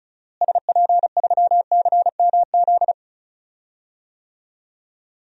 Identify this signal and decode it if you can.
Morse code, heard as a keyed tone, about 34 wpm, 700 Hz. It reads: SP3CMZ
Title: DXCC-100 Call Signs - 6 Characters - Repeated 2x - 35wpm